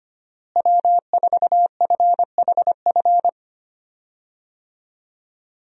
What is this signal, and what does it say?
Morse code, heard as a keyed tone, about 25 wpm, 700 Hz. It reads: W4FHF
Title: USA Call Signs - 5 Characters - Repeated 3x - 25wpm